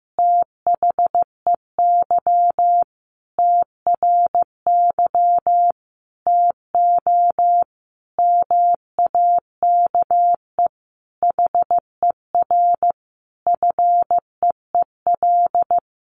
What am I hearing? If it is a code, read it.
Morse code, heard as a keyed tone, about 15 wpm, 700 Hz. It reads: THEY TRY TO MAKE HER FEEL
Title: Easy Phrases 15wpm